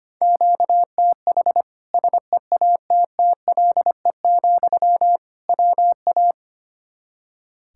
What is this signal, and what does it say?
Morse code, heard as a keyed tone, about 25 wpm, 700 Hz. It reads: QTH SEATTLE, WA
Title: Practice QSO - Exchange 1 - Beginning - 25wpm